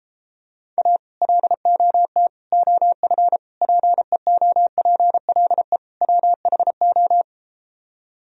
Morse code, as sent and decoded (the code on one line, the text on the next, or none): .- / .-.. --- - / --- ..-. / .--. --- .--. .-.. / .-- .... ---
A LOT OF PEOPLE WHO